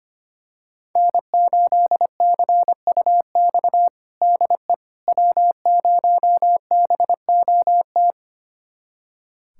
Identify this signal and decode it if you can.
Morse code, heard as a keyed tone, about 25 wpm, 700 Hz. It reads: N8CUX DE W0BOT
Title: USA Callsign DE Callsign - 5 Characters - 25wpm